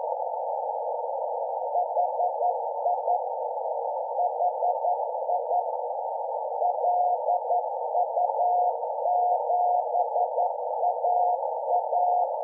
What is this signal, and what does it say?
Morse code, heard as a keyed tone, about 10 wpm, 720 Hz, with noise. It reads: HI HI LU7AA
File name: lu4aao / HDSDR Hz AF